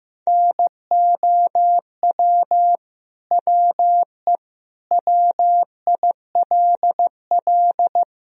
Morse code, heard as a keyed tone, about 15 wpm, 700 Hz. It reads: NOW WE WILL
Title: Sentences from Top 1000 Words - Review Part 1 - 15wpm